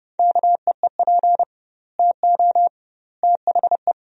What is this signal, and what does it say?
Morse code, heard as a keyed tone, about 30 wpm, 700 Hz. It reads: KEEP TO THE
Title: 2x Word-Spacing - Sentences from Top 300 Words - 30wpm